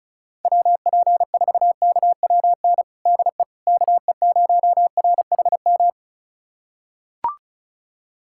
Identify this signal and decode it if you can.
Morse code, heard as a keyed tone, about 36 wpm, 700 Hz. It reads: WP4KWN DE KE0RHM
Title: USA Callsign DE Callsign - 6 Characters - 35wpm